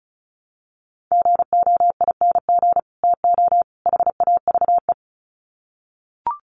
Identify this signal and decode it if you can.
Morse code, heard as a keyed tone, about 34 wpm, 700 Hz. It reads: GOING TO HAVE
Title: Sets of 3 Words 35wpm